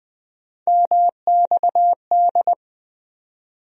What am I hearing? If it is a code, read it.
Morse code, heard as a keyed tone, about 21 wpm, 700 Hz. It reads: MXD
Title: USA Call Sign Postfixes - 3 Letters 20wpm